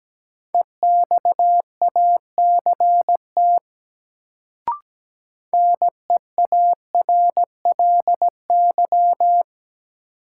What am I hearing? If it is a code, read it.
Morse code, heard as a keyed tone, about 17 wpm, 700 Hz. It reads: EXACT NEARLY
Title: Sentences from Top 1300 Words - New Word Drill - 17wpm